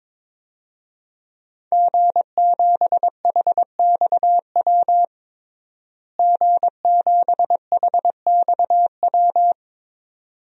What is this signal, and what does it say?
Morse code, heard as a keyed tone, about 22 wpm, 700 Hz. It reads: G7HXW G7HXW
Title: DXCC-100 Call Signs - 5 Characters - Repeated 2x - 22wpm